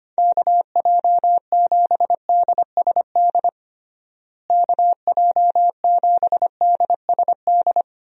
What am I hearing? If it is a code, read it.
Morse code, heard as a keyed tone, about 25 wpm, 700 Hz. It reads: KJ7DSD KJ7DSD
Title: USA Call Signs - 6 Characters - Repeated 3x - 25wpm